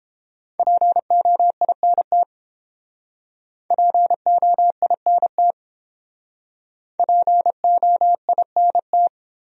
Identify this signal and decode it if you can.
Morse code, heard as a keyed tone, about 31 wpm, 700 Hz. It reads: POINT POINT POINT